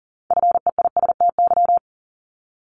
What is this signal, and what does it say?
Morse code, heard as a keyed tone, about 39 wpm, 700 Hz. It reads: FEISTY
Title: Multi-letter Phonemes 40wpm